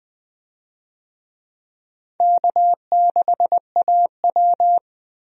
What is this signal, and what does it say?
Morse code, heard as a keyed tone, about 20 wpm, 700 Hz. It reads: K6AW
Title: USA Call Signs - 4 Characters - 20wpm